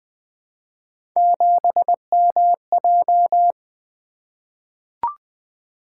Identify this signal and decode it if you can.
Morse code, heard as a keyed tone, about 20 wpm, 700 Hz. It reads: 7MJ
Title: Three Letter-Number Combinations - 20wpm